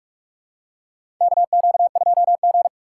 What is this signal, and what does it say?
Morse code, heard as a keyed tone, about 42 wpm, 700 Hz. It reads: KQ2G
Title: USA Call Signs - 3x Character Spacing - 45wpm